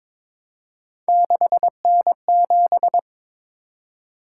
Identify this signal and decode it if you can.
Morse code, heard as a keyed tone, about 22 wpm, 700 Hz. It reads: 6N7
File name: practice / 22wpm / Three Letter-Number Combinations - 22wpm